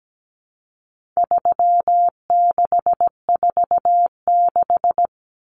Morse code, heard as a keyed tone, about 17 wpm, 700 Hz. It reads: 3646
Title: Common Cut Numbers - 4-Digit 17wpm